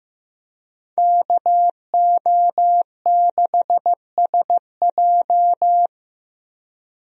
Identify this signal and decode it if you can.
Morse code, heard as a keyed tone, about 15 wpm, 700 Hz. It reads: KO6SJ
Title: USA Call Signs - 5 Characters - Repeated 2x - 15wpm